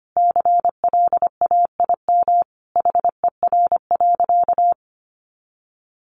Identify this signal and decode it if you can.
Morse code, heard as a keyed tone, about 25 wpm, 700 Hz. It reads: CLAIM HER.
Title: Sentences from Top 1500 Words 25wpm